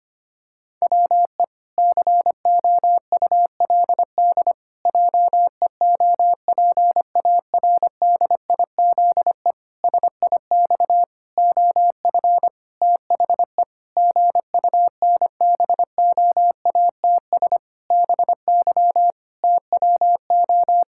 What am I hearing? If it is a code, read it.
Morse code, heard as a keyed tone, about 25 wpm, 700 Hz. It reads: WE COULD JEOPARDIZE SIX OF THE GUNBOATS BY TWO
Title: Pangram Challenge v2 - 25wpm